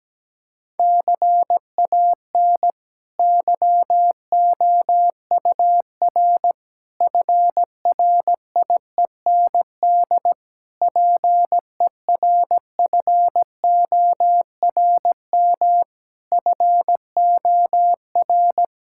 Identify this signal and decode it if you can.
Morse code, heard as a keyed tone, about 17 wpm, 700 Hz. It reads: CAN YOUR FRIEND PERFORM FOR
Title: Sentences from Top 1000 Words - Review Part 5 - 17wpm